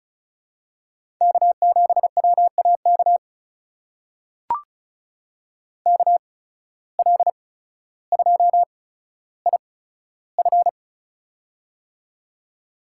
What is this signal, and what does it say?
Morse code, heard as a keyed tone, about 36 wpm, spaced as for 15 wpm, 700 Hz. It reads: K7WAK KL2IF
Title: USA Call Signs - 5 Characters - 8x Character Spacing - 35wpm